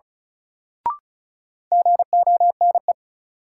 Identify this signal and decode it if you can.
Morse code, heard as a keyed tone, about 36 wpm, 700 Hz. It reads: GONE
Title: Sentences from Top 1200 Words - New Word Drill - 35wpm